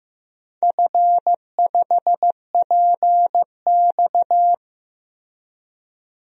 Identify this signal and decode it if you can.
Morse code, heard as a keyed tone, about 15 wpm, 700 Hz. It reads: F5PX
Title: DXCC-100 Call Signs - 4 Characters - 15wpm